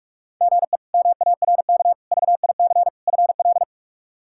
Morse code, heard as a keyed tone, about 42 wpm, 700 Hz. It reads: GE MARK VIC FL
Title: K1USN SST QSO Practice - 45wpm